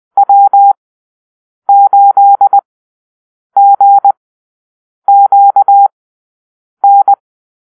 Morse code, heard as a keyed tone, about 20 wpm, spaced as for 7 wpm, 800 Hz. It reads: W8GQN
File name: arrl / Call-Sign Drill-10